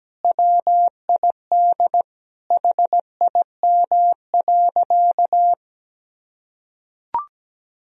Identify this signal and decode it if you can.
Morse code, heard as a keyed tone, about 17 wpm, 700 Hz. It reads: WID HIM.
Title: Sentences from Top 300 Words and Abbreviations 17wpm